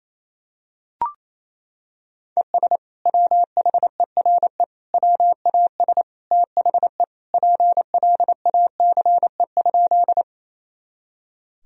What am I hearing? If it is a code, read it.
Morse code, heard as a keyed tone, about 28 wpm, 700 Hz. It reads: ES WHERE WAS THE PLACE?
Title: Sentences from Top 200 Words and Abbreviations 28wpm